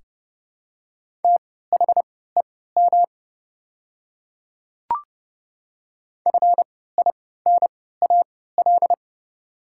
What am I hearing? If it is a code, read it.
Morse code, heard as a keyed tone, about 30 wpm, spaced as for 17 wpm, 700 Hz. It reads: THEM FINAL
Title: 3x Character Spacing - Top 500 Words - 30wpm